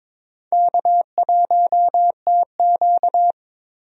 Morse code, heard as a keyed tone, about 21 wpm, 700 Hz. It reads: K1TQ
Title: USA Call Signs - 4 Characters - Repeated 2x - 22wpm